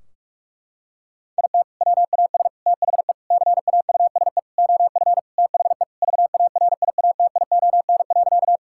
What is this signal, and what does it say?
Morse code, heard as a keyed tone, about 41 wpm, 700 Hz. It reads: IT WAS THE CAUSE OF THE VARIATION.
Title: Sentences from Top 1600 Words 45wpm